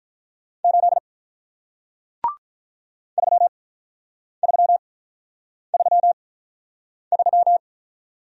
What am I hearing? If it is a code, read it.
Morse code, heard as a keyed tone, about 48 wpm, spaced as for 9 wpm, 700 Hz. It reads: 8 3333